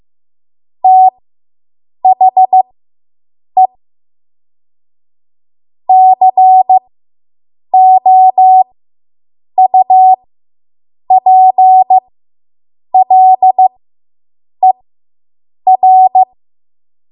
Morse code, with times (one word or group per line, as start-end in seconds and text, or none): 0.8-3.7 THE
5.9-16.2 COUPLER